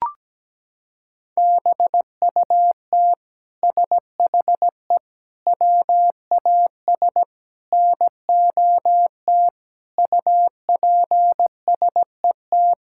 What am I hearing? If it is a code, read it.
Morse code, heard as a keyed tone, about 17 wpm, 700 Hz. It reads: BUT SHE WAS NOT UPSET
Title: Sentences from Top 1000 Words - Review Part 5 - 17wpm